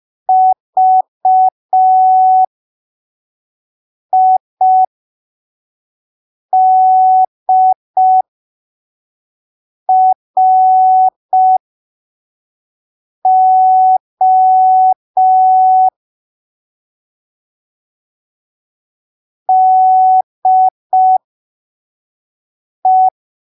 Morse code, heard as a keyed tone, about 5 wpm, spaced as for 3 wpm, 750 Hz. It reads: VIDRO DE
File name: arquivos / texto / texto 11 05ppm